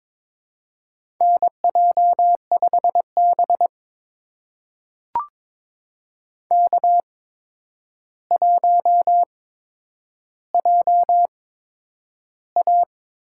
Morse code, heard as a keyed tone, about 22 wpm, spaced as for 9 wpm, 700 Hz. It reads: NJ5B K1JA
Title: USA Call Signs - 4 Characters - 8x Character Spacing - 22wpm